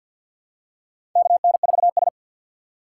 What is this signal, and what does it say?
Morse code, heard as a keyed tone, about 53 wpm, 700 Hz. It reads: KN4S
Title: USA Call Signs - 4 Characters - 3x Character Spacing - 50wpm